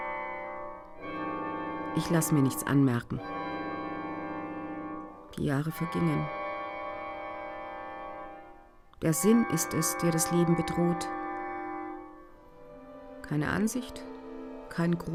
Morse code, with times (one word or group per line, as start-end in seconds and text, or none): none